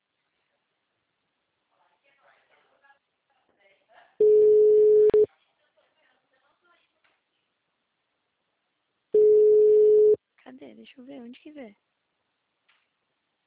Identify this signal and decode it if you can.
Ringback tone (Europe, 425 Hz), with speech.